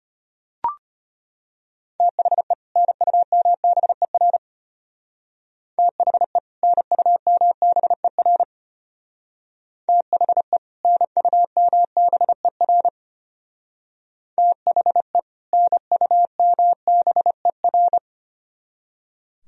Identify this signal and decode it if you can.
Morse code, heard as a keyed tone, about 30 wpm, 700 Hz. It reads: THE NUMBER THE NUMBER THE NUMBER THE NUMBER
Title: Speed-Racing - Sets of 2 Words 25wpm